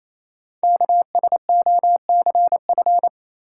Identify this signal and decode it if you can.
Morse code, heard as a keyed tone, about 27 wpm, 700 Hz. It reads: KSOCF
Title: Code Groups - 5 Letter:Numbers - 28wpm